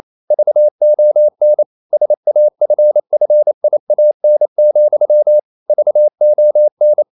Morse code, heard as a keyed tone, about 28 wpm, 600 Hz. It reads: VON SAFFIAN, VON
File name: Liebespaar4 28wpm